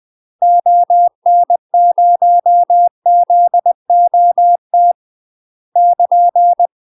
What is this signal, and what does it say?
Morse code, heard as a keyed tone, about 20 wpm, 700 Hz. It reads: ON0ZOT <KN>
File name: UK MASTER QSOs TRAINING / 20WPM / UK MASTER QSOs 0003